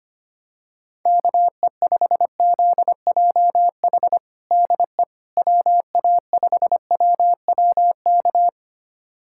Morse code, heard as a keyed tone, about 25 wpm, 700 Hz. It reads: KE5ZJH DE WA5WWK
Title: USA Callsign DE Callsign - 6 Characters - 25wpm